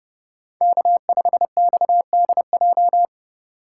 Morse code, heard as a keyed tone, about 30 wpm, 700 Hz. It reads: K5XDJ
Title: USA Call Signs - 5 Characters - 2x Character Spacing - 30wpm